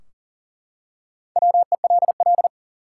Morse code, heard as a keyed tone, about 42 wpm, 700 Hz. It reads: WELL
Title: Top 180 QSO Elements 40wpm